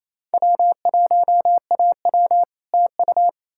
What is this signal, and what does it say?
Morse code, heard as a keyed tone, about 38 wpm, 700 Hz. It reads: W1AW TU